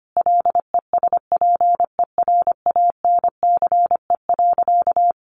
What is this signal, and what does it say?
Morse code, heard as a keyed tone, about 25 wpm, 700 Hz. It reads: LESPERANCE.
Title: A Sound Of Thunder 25wpm